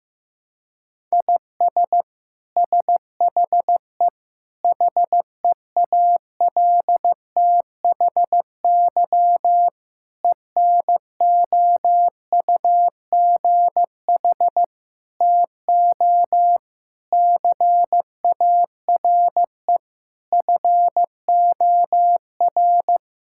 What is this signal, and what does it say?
Morse code, heard as a keyed tone, about 16 wpm, 700 Hz. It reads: IS SHE HEALTHY ENOUGH TO CARE FOR
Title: Sentences from Top 1000 Words - Review Part 5 - 15wpm